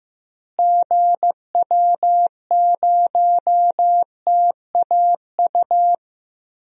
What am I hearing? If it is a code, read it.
Morse code, heard as a keyed tone, about 15 wpm, 700 Hz. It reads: GW0TAU
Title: DXCC-100 Call Signs - 6 Characters - Repeated 3x - 15wpm